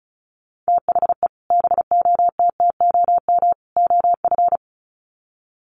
Morse code, heard as a keyed tone, about 34 wpm, 700 Hz. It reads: THE BOTTOM OF
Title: Sets of 3 Words 35wpm